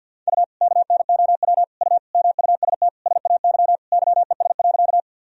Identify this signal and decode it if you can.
Morse code, heard as a keyed tone, about 52 wpm, 700 Hz. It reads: U KNOW U MUST SAY YES.
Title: Sentences from Top 500 Words and Abbreviations 50wpm